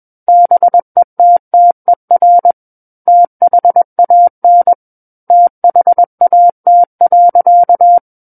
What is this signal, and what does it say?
Morse code, heard as a keyed tone, about 21 wpm, 700 Hz. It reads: BETTER THAN THAT.
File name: Book One / 12-Chapter 12 - 21 WPM